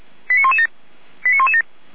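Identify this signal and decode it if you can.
Five-tone selective call (ZVEI), heard as quick runs of five tones, tones of about 70 ms. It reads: 89117 89117
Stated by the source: Five-tone ZVEI selective call 89084, received on SDR.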